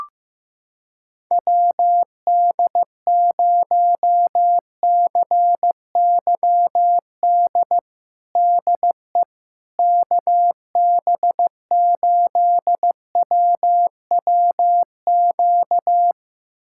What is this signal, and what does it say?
Morse code, heard as a keyed tone, about 15 wpm, 700 Hz. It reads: WD0CYD DE KB8WWQ
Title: USA Callsign DE Callsign - 6 Characters - 15wpm